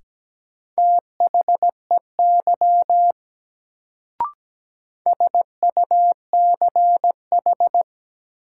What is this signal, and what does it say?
Morse code, heard as a keyed tone, about 17 wpm, 700 Hz. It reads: THEY SUCH
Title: Sets of 2 Words - New Word Drill - 17wpm